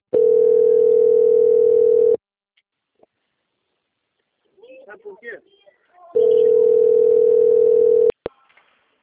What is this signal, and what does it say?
Ringback tone (North America), 440 + 480 Hz, with speech.
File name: out- - - - - .8394